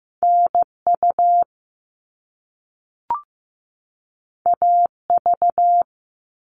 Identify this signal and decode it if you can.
Morse code, heard as a keyed tone, about 15 wpm, 700 Hz. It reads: NU AV
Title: All Cut Numbers - 2-Digit 15wpm